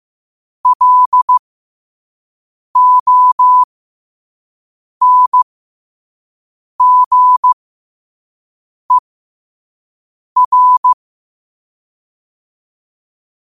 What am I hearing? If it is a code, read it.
Morse code, heard as a keyed tone, about 15 wpm, spaced as for 5 wpm, 1000 Hz. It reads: LONGER
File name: qotdfiles / 5wpm